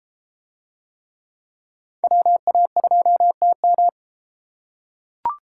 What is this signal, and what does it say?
Morse code, heard as a keyed tone, about 32 wpm, 700 Hz. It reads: WA2TM